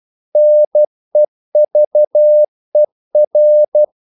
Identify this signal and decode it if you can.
Morse code, heard as a keyed tone, about 12 wpm, 600 Hz. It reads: NEVER
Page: English quotations in CW (telegraphy - morse)